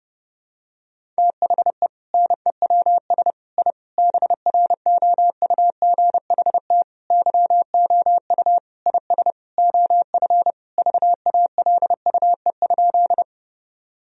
Morse code, heard as a keyed tone, about 30 wpm, 700 Hz. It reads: THE NEWS I BROUGHT YOU IS OF VALUE?
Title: Sentences from Top 1000 Words - Review Part 3 - 30wpm